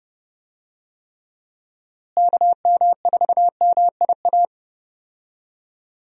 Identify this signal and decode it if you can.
Morse code, heard as a keyed tone, about 30 wpm, 700 Hz. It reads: KM4MIA
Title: USA Call Signs - 6 Characters - Repeated 3x - 30wpm